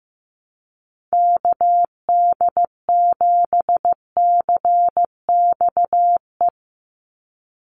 Morse code, heard as a keyed tone, about 15 wpm, 700 Hz. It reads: KD7CXE
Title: USA Call Signs 15wpm